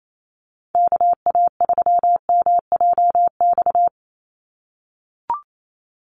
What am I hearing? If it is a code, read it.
Morse code, heard as a keyed tone, about 28 wpm, 700 Hz. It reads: KA3MJX
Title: USA Call Signs 28wpm